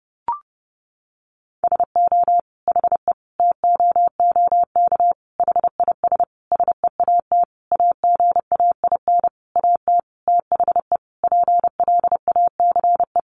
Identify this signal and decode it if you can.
Morse code, heard as a keyed tone, about 30 wpm, 700 Hz. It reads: SO HE TOOK HIS SEAT AGAIN AT THE PLACE
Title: Sentences from Top 1500 Words 30wpm